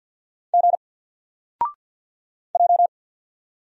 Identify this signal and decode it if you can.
Morse code, heard as a keyed tone, about 50 wpm, 700 Hz. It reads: G J
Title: Single Letter-Number 50wpm